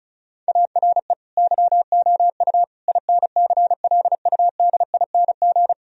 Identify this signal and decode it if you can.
Morse code, heard as a keyed tone, about 36 wpm, 700 Hz. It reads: ARE YOU INCLUDING
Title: Sentences from Top 1000 Words - Review Part 4 - 35wpm